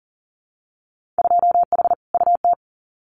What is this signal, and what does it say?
Morse code, heard as a keyed tone, about 40 wpm, 700 Hz. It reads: 2H UT
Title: Winter Field Day Exchanges Encore 40wpm